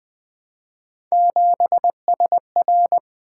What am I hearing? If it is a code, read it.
Morse code, heard as a keyed tone, about 20 wpm, 700 Hz. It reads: 7SR
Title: Three Letter-Number Combinations - 20wpm